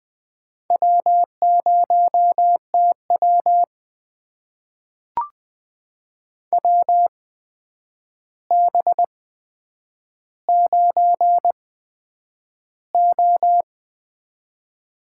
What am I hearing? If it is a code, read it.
Morse code, heard as a keyed tone, about 20 wpm, spaced as for 9 wpm, 700 Hz. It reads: W0TW WB9O
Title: USA Call Signs - 8x Character Spacing - 20wpm